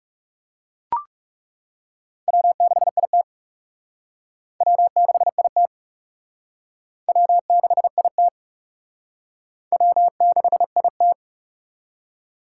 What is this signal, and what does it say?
Morse code, heard as a keyed tone, about 39 wpm, 700 Hz. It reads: W6IT W6IT W6IT W6IT